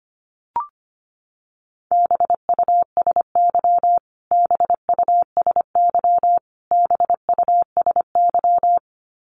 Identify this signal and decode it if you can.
Morse code, heard as a keyed tone, about 25 wpm, 700 Hz. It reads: BUSY BUSY BUSY
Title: Top 500 Words Repeated 25wpm